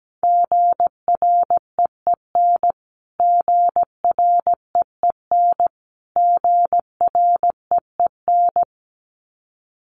Morse code, heard as a keyed tone, about 17 wpm, 700 Hz. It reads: GREEN GREEN GREEN
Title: Top 500 Words Repeated 17wpm